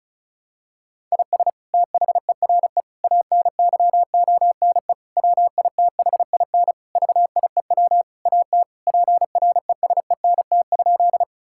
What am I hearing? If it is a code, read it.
Morse code, heard as a keyed tone, about 34 wpm, 700 Hz. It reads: IS THERE ANYONE WITHIN VIEW AT PRESENT?
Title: Sentences from Top 1000 Words - Review Part 6 - 35wpm